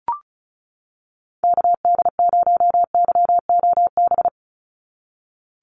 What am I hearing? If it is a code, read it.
Morse code, heard as a keyed tone, about 34 wpm, 700 Hz. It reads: KD0YOB
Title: NA SOTA Activator Callsigns 35wpm